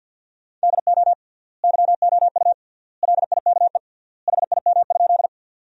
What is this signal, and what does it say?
Morse code, heard as a keyed tone, about 50 wpm, 700 Hz. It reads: DO YOU LIKE HIM?
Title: 3x Word-Spacing - Sentences from Top 100 Words - 50wpm